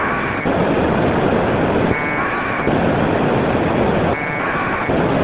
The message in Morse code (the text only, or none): none